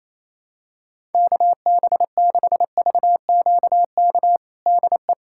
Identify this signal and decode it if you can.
Morse code, heard as a keyed tone, about 28 wpm, 700 Hz. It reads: KB6VQK DE
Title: USA Callsign DE Callsign - 6 Characters - 28wpm